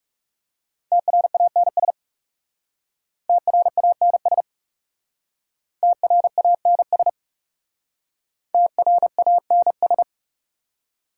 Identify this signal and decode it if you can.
Morse code, heard as a keyed tone, about 38 wpm, 700 Hz. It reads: TRANS TRANS TRANS TRANS